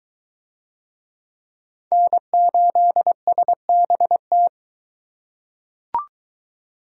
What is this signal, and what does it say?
Morse code, heard as a keyed tone, about 22 wpm, 700 Hz. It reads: N8SBT